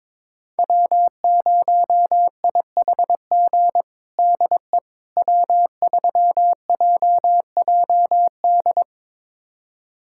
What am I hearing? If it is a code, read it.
Morse code, heard as a keyed tone, about 22 wpm, 700 Hz. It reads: W0IHG DE W3JJD
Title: USA Callsign DE Callsign - 5 Characters - 22wpm